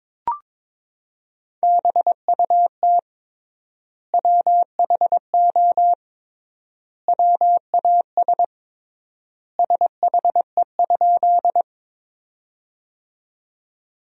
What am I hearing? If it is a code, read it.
Morse code, heard as a keyed tone, about 22 wpm, 700 Hz. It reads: BUT WHO WAS SHE?
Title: 3x Word-Spacing - Sentences from Top 100 Words - 22wpm